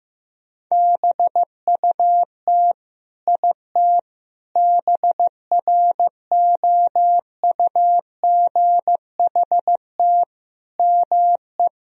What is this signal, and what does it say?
Morse code, heard as a keyed tone, about 15 wpm, 700 Hz. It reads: BUT IT BROUGHT ME
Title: Sentences from Top 1000 Words - Review Part 4 - 15wpm